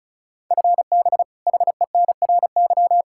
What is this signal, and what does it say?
Morse code, heard as a keyed tone, about 34 wpm, 700 Hz. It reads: FB HENRY
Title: Practice QSO - Exchange 3 - More Info - 35wpm